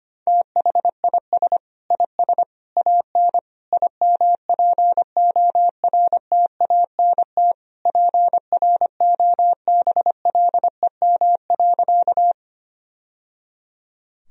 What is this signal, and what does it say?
Morse code, heard as a keyed tone, about 25 wpm, 700 Hz. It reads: THIS IS AN IMPORTANT PROBLEM.
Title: Sentences from Top 1000 Words - Review Part 2 - 25wpm